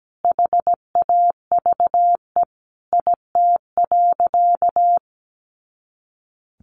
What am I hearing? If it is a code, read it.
Morse code, heard as a keyed tone, about 17 wpm, 700 Hz. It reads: HAVE IT.
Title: Sentences from Top 100 Words 17wpm